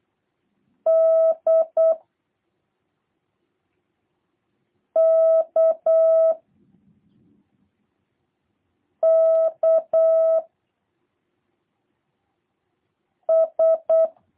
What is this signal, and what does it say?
Morse code, heard as a keyed tone, about 8 wpm, 650 Hz. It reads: D K K S
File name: jh8cbh / 2023cw / review1-3